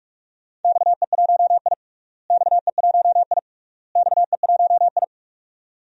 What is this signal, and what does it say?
Morse code, heard as a keyed tone, about 42 wpm, 700 Hz. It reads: XE1I XE1I XE1I
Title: DXCC-100 Call Signs - 4 Characters - Repeated 3x - 45wpm